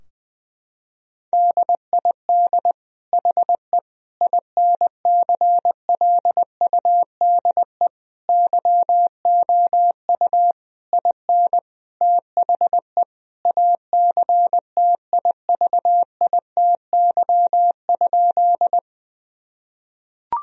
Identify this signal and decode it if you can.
Morse code, heard as a keyed tone, about 20 wpm, 700 Hz. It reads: DID HE INCLUDE YOU IN THE ACTIVITY?
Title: Sentences from Top 1000 Words - Review Part 4 - 20wpm